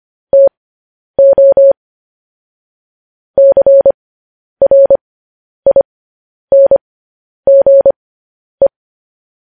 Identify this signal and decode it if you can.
Morse code, heard as a keyed tone, about 25 wpm, spaced as for 10 wpm, 550 Hz. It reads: TO CRINGE
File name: podcast / News headlines-1x-25x10